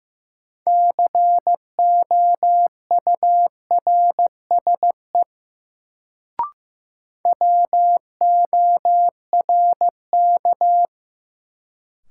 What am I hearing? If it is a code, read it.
Morse code, heard as a keyed tone, about 16 wpm, 700 Hz. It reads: COURSE WORK
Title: Sets of 2 Words - New Word Drill - 15wpm